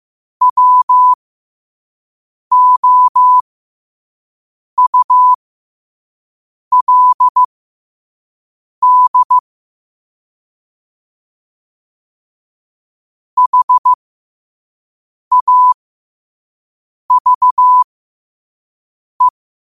Morse code, heard as a keyed tone, about 15 wpm, spaced as for 5 wpm, 1000 Hz. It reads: WOULD HAVE
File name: 5wpm